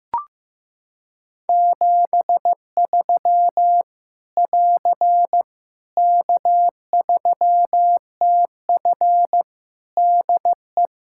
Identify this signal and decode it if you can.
Morse code, heard as a keyed tone, about 15 wpm, 700 Hz. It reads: 73 <AR> K3TF DE
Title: Practice QSO - Exchange 4 - Ending - 15wpm